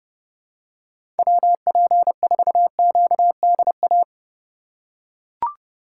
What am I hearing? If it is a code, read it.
Morse code, heard as a keyed tone, about 30 wpm, 700 Hz. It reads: WP4QDA
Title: USA Call Signs - 6 Characters - 2x Character Spacing - 30wpm